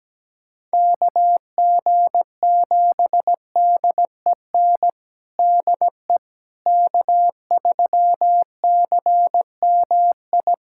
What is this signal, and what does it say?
Morse code, heard as a keyed tone, about 17 wpm, 700 Hz. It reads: KG7DEN DE K3CMI
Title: USA Callsign DE Callsign 17wpm